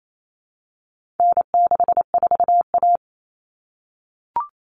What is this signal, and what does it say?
Morse code, heard as a keyed tone, about 28 wpm, 700 Hz. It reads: N64A
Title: Common Cut Numbers - 4-Digit 28wpm